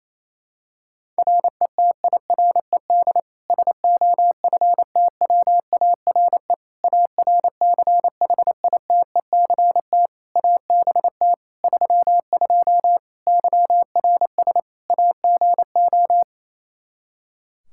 Morse code, heard as a keyed tone, about 28 wpm, 700 Hz. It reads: RETIRED SOFTWARE ARCHITECT ABT 32 YRS AGO
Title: Practice QSO - Exchange 3 - More Info - 28wpm